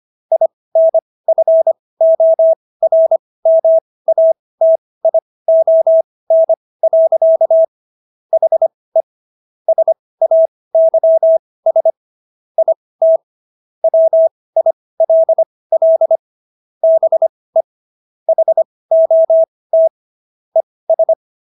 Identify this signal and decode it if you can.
Morse code, heard as a keyed tone, about 25 wpm, spaced as for 18 wpm, 650 Hz. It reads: INFORMATION. HE SAYS IT WILL BE HOT ES